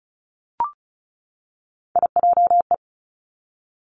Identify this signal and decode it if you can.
Morse code, heard as a keyed tone, about 36 wpm, 700 Hz. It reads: IJE
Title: Three Challenging Letter-Number Combinations 35wpm